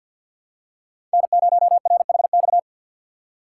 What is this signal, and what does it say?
Morse code, heard as a keyed tone, about 47 wpm, 700 Hz. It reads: N0RHX
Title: USA Call Signs - 5 Characters - 4x Character Spacing - 50wpm